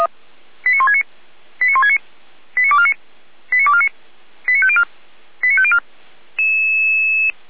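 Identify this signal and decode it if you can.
Five-tone selective call (ZVEI), heard as quick runs of five tones, tones of about 70 ms. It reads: 89169 89169 89249 89249 89504 89504
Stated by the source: ZVEI five-tone call received by SDR, fire brigade Feuerwehr Orlamünde.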